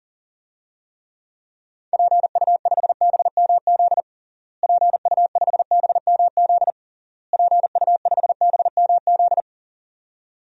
Morse code, heard as a keyed tone, about 40 wpm, 700 Hz. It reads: PU5BMZ PU5BMZ PU5BMZ
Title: DXCC-100 Call Signs - 6 Characters - Repeated 3x - 40wpm